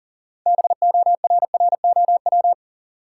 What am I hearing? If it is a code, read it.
Morse code, heard as a keyed tone, about 40 wpm, 700 Hz. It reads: BORROW
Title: Sentences from Top 1500 Words - New Word Drill - 40wpm